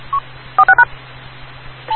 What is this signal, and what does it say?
DTMF, heard as telephone touch tones, tones of about 100 ms. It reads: 46#
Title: Rocky Mountain Radio League Repeaters